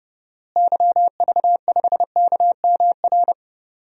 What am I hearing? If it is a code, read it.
Morse code, heard as a keyed tone, about 32 wpm, 700 Hz. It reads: YV5KMR
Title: DXCC-100 Call Signs - 6 Characters - Repeated 3x - 30wpm